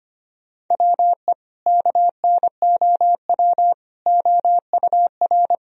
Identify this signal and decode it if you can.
Morse code, heard as a keyed tone, about 25 wpm, 700 Hz. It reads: WE KNOW OUR
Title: Sentences from Top 1000 Words - Review Part 4 - 25wpm